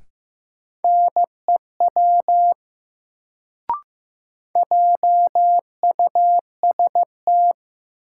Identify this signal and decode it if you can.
Morse code, heard as a keyed tone, about 15 wpm, 700 Hz. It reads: NEW JUST
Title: Sets of 2 Words - New Word Drill - 15wpm